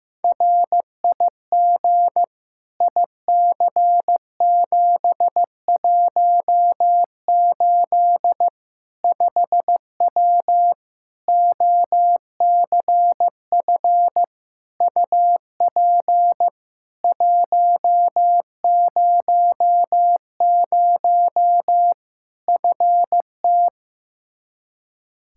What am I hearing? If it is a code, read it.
Morse code, heard as a keyed tone, about 16 wpm, 700 Hz. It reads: RIG IC718 5W OCF UP 100 FT
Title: K1USN SST QSO Practice 15wpm